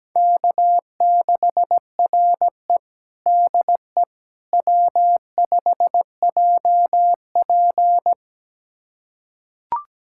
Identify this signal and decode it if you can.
Morse code, heard as a keyed tone, about 17 wpm, 700 Hz. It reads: K6RE DE W5JP
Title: USA Callsign DE Callsign - 4 Characters - 17wpm